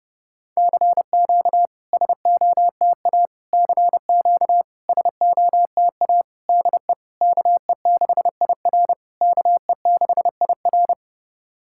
Morse code, heard as a keyed tone, about 30 wpm, 690 Hz. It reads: CQ SOTA CQ SOTA DE KE6IR KE6IR
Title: SOTA QSO Practice 30wpm